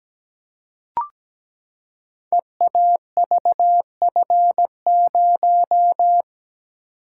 Morse code, heard as a keyed tone, about 17 wpm, 700 Hz. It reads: EAVF0
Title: Code Groups - 5 Letter:Numbers - 17wpm